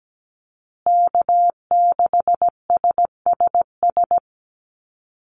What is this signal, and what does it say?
Morse code, heard as a keyed tone, about 17 wpm, 700 Hz. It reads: K6SSS
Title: Call Signs Repeated 3x - 17wpm